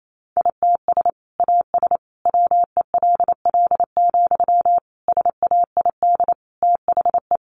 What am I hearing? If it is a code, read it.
Morse code, heard as a keyed tone, about 28 wpm, 700 Hz. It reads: ITS AS WELL, SAID THE
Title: Sentences from Top 600 Words 28wpm